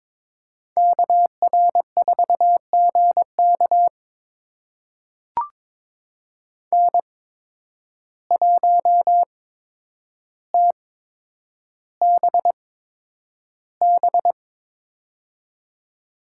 Morse code, heard as a keyed tone, about 22 wpm, spaced as for 9 wpm, 700 Hz. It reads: KR4GK N1TBB
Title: USA Call Signs - 5 Characters - 8x Character Spacing - 22wpm